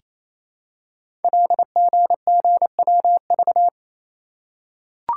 Morse code, heard as a keyed tone, about 28 wpm, 700 Hz. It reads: LGGWV
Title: Code Groups - 5 Letters - 28wpm